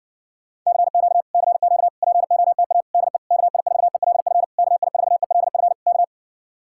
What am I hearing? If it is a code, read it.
Morse code, heard as a keyed tone, about 60 wpm, 700 Hz. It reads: CQ CQ POTA DE KI4ELF KI4ELF K